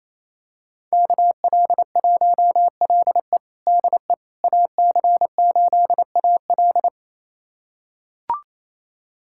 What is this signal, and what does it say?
Morse code, heard as a keyed tone, about 28 wpm, 690 Hz. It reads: KL1LE DE AC8AL
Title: USA Callsign DE Callsign - 5 Characters - 28wpm